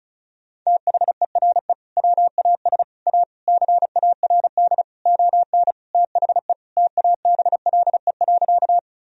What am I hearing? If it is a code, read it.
Morse code, heard as a keyed tone, about 34 wpm, 700 Hz. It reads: THERE WAS A CARD ON THE TABLE.
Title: Sentences from Top 1000 Words - Review Part 4 - 35wpm